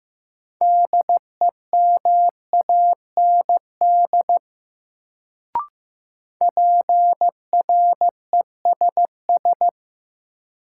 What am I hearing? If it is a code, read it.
Morse code, heard as a keyed tone, about 15 wpm, 700 Hz. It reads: DEMAND PRESS
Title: Sentences from Top 800 Words - New Word Drill - 15wpm